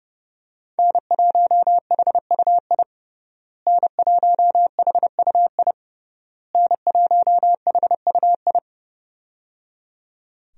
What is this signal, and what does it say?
Morse code, heard as a keyed tone, about 30 wpm, 700 Hz. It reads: N1HUI N1HUI N1HUI
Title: USA Call Signs - 5 Characters - Repeated 3x - 30wpm